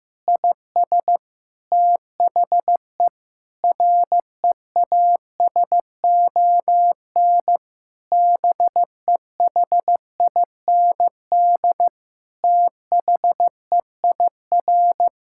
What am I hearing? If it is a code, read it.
Morse code, heard as a keyed tone, about 15 wpm, 700 Hz. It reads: IS THE REASON BEHIND THEIR
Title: Sentences from Top 1000 Words - Review Part 5 - 15wpm